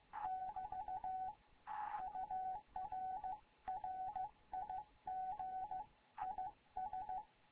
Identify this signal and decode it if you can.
Morse code, heard as a keyed tone, about 15 wpm, 750 Hz. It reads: <BT> UR RIG IS
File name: Morze-HA2MN / hangp / 2 11 GYAKORLAT